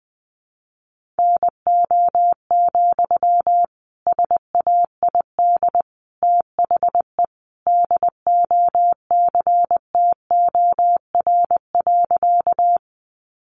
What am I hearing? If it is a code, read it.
Morse code, heard as a keyed tone, about 20 wpm, 700 Hz. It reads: NO, SAID THE DOCTOR.
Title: Sentences from Top 1500 Words 20wpm